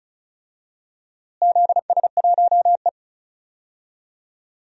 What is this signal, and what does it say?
Morse code, heard as a keyed tone, about 36 wpm, 700 Hz. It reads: ZS1E
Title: DXCC-100 Call Signs - 4 Characters - Repeated 3x - 35wpm